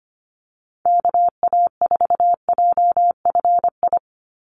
Morse code, heard as a keyed tone, about 25 wpm, 700 Hz. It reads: KA4JFI
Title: USA Call Signs 25wpm